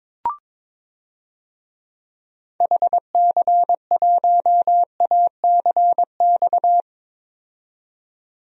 Morse code, heard as a keyed tone, about 22 wpm, 700 Hz. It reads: HC1ACX
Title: DXCC-100 Call Signs - 6 Characters - 22wpm